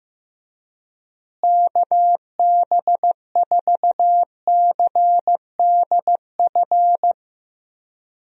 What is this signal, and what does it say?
Morse code, heard as a keyed tone, about 15 wpm, 700 Hz. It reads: KB4CDF
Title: USA Call Signs - 6 Characters - Repeated 2x - 15wpm